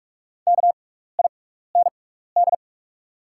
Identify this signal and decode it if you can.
Morse code, heard as a keyed tone, about 46 wpm, spaced as for 16 wpm, 690 Hz. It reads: KIND
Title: 6x Character Spacing - Top 200 Words - 45wpm